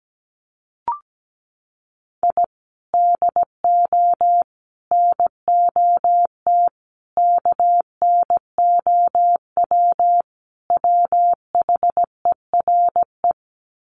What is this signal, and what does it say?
Morse code, heard as a keyed tone, about 17 wpm, 700 Hz. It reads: I DO NOT KNOW WHERE
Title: Sets of 5 Words 17wpm